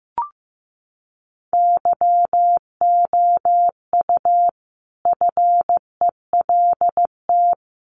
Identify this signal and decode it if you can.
Morse code, heard as a keyed tone, about 15 wpm, 700 Hz. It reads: YOU FELT
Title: A Sound Of Thunder 15wpm